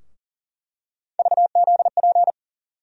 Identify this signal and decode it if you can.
Morse code, heard as a keyed tone, about 40 wpm, 700 Hz. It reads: VZP
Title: USA Call Sign Postfixes - 3 Letters 40wpm